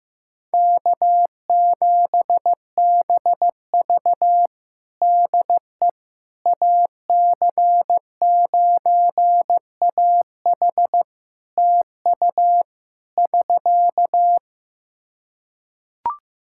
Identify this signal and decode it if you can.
Morse code, heard as a keyed tone, about 14 wpm, 700 Hz. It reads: K7BV DE AC9AH TU <SK>
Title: Practice QSO - Exchange 4 - Ending - 15wpm